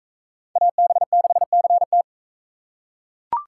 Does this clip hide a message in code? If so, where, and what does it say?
Yes: Morse code from 0.6 to 2.0 s, heard as a keyed tone, about 41 wpm, 700 Hz. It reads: AB6CT